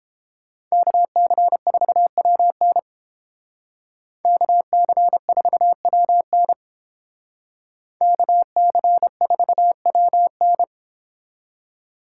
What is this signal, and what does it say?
Morse code, heard as a keyed tone, about 28 wpm, 700 Hz. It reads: KC4WN KC4WN KC4WN